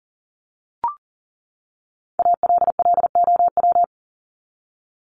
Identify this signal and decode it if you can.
Morse code, heard as a keyed tone, about 40 wpm, 700 Hz. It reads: ALLOW
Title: Top 501-750 Words 40wpm